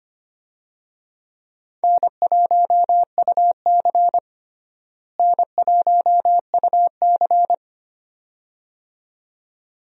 Morse code, heard as a keyed tone, about 25 wpm, 700 Hz. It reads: N1UC N1UC
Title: USA Call Signs - 4 Characters - Repeated 2x - 25wpm